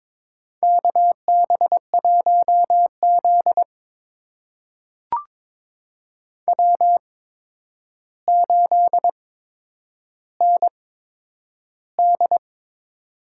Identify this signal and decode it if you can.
Morse code, heard as a keyed tone, about 22 wpm, spaced as for 9 wpm, 700 Hz. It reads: KB1Z W8ND
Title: USA Call Signs - 4 Characters - 8x Character Spacing - 22wpm